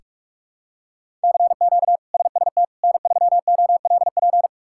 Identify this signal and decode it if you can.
Morse code, heard as a keyed tone, about 41 wpm, 700 Hz. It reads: CQ SST N3OLP
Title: K1USN SST QSO Practice - 45wpm